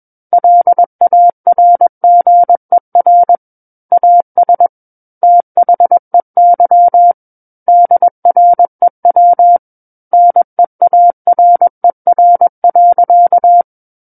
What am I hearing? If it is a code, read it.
Morse code, heard as a keyed tone, about 21 wpm, 700 Hz. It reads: LARGER AS THEY DREW NEARER.
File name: Book One / 12-Chapter 12 - 21 WPM